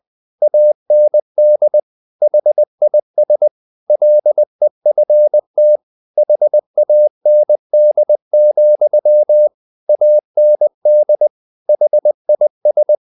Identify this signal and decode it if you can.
Morse code, heard as a keyed tone, about 20 wpm, 600 Hz. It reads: AND HIS LEFT HAND, AND HIS
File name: the old man and the sea part 2 20wpm